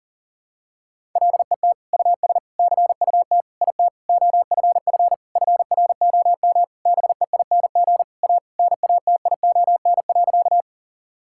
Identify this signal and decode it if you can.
Morse code, heard as a keyed tone, about 40 wpm, 700 Hz. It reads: LET US CUT IT OFF FROM BEING A NATION.
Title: Sentences from Top 1000 Words - Review Part 6 - 40wpm